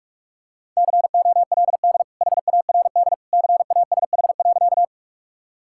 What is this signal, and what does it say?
Morse code, heard as a keyed tone, about 45 wpm, 700 Hz. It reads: COLD HARD CASH.
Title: Sentences from Top 1000 Words - Review Part 5 - 45wpm